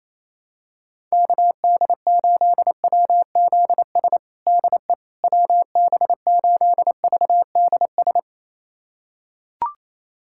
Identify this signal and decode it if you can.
Morse code, heard as a keyed tone, about 28 wpm, 700 Hz. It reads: KD8WZS DE WB8VDS
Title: USA Callsign DE Callsign - 6 Characters - 28wpm